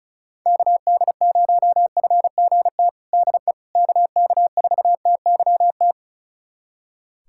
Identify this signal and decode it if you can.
Morse code, heard as a keyed tone, about 36 wpm, 700 Hz. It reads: KD0FGT DE KK4TYT
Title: USA Callsign DE Callsign - 6 Characters - 35wpm